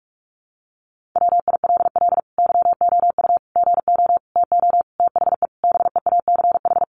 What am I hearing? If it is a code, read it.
Morse code, heard as a keyed tone, about 44 wpm, 700 Hz. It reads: WILL YOU GO TO THE BEACH
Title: Sentences from Top 1500 Words 45wpm